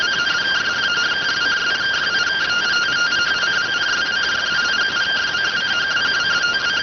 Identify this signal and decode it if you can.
Two-tone teleprinter signal, about 170 Hz shift.